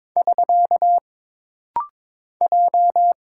Morse code, heard as a keyed tone, about 22 wpm, 700 Hz. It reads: <SK> J
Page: This rapid-fire timing is geared to push your ICR (Instant Character Recognition) skills.